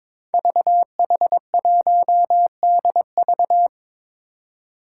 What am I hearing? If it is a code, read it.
Morse code, heard as a keyed tone, about 22 wpm, 700 Hz. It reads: VH1DV
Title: Five Challenging Letter-Number Combinations 22wpm